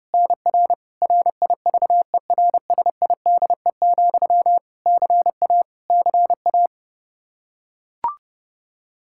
Morse code, heard as a keyed tone, about 30 wpm, 700 Hz. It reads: NR RIVERSIDE, CA CA
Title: Practice QSO - Exchange 1 - Beginning - 30wpm